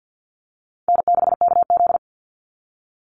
Morse code, heard as a keyed tone, about 49 wpm, 700 Hz. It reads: N6KZ
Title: NA SOTA Chaser Callsigns 50wpm